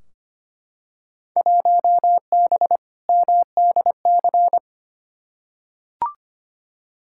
Morse code, heard as a keyed tone, about 25 wpm, 700 Hz. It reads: 1B MDC
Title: Field Day Exchanges Encore - 25wpm